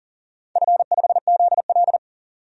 Morse code, heard as a keyed tone, about 40 wpm, 700 Hz. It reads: F5ZL
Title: DXCC-100 Call Signs - 4 Characters - 40wpm